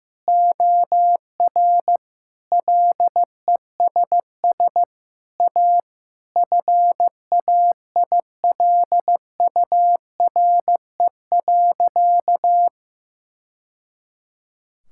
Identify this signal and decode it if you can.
Morse code, heard as a keyed tone, about 15 wpm, 700 Hz. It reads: OR LESS A FAILURE.
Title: Sentences from Top 1000 Words - Review Part 6 - 15wpm